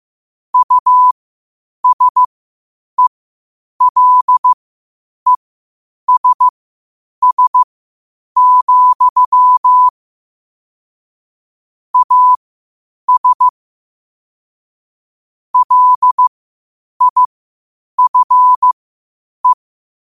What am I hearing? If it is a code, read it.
Morse code, heard as a keyed tone, about 15 wpm, spaced as for 8 wpm, 1000 Hz. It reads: USELESS, AS LIFE
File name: qotdfiles / 8wpm